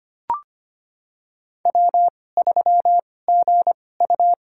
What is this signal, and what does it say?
Morse code, heard as a keyed tone, about 25 wpm, spaced as for 18 wpm, 700 Hz. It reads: W3GU
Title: USA Call Signs - 4 Characters - 2x Character Spacing - 25wpm